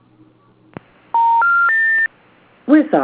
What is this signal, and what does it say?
Special information tone, with speech.